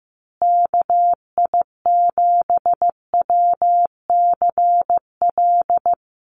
Morse code, heard as a keyed tone, about 15 wpm, 700 Hz. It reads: KI7WCL
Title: USA Call Signs Repeated 3x 15wpm